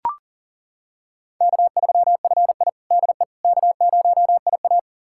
Morse code, heard as a keyed tone, about 40 wpm, 700 Hz. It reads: K3FI DE K0IA
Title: USA Callsign DE Callsign - 4 Characters - 40wpm